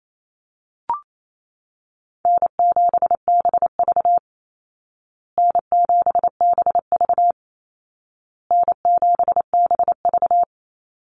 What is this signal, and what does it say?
Morse code, heard as a keyed tone, about 28 wpm, 700 Hz. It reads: N7BV N7BV N7BV
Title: Easy Call Signs Repeated 3x - 28wpm